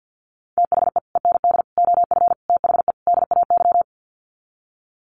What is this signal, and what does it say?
Morse code, heard as a keyed tone, about 48 wpm, 700 Hz. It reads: THE END OF THE DAY
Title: Sets of 5 Words 50wpm